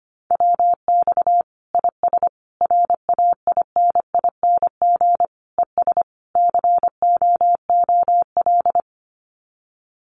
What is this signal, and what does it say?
Morse code, heard as a keyed tone, about 25 wpm, 700 Hz. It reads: WX IS RAINING ES COOL
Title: Common QSO Phrases 25wpm